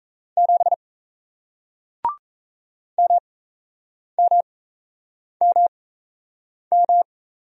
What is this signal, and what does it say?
Morse code, heard as a keyed tone, about 38 wpm, spaced as for 8 wpm, 700 Hz. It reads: 7 MMMM